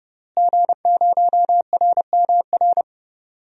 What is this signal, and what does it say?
Morse code, heard as a keyed tone, about 31 wpm, 700 Hz. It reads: G0RMR
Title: DXCC-100 Call Signs - 5 Characters - Repeated 3x - 30wpm